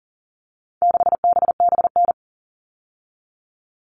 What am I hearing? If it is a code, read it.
Morse code, heard as a keyed tone, about 40 wpm, 700 Hz. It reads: 6BBN
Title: All Cut Numbers - 4-Digit 40wpm